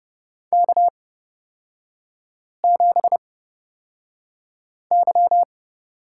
Morse code, heard as a keyed tone, about 30 wpm, spaced as for 5 wpm, 690 Hz. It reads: K7Y
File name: Lesson 38e - Letter q - Three Characters 30wpm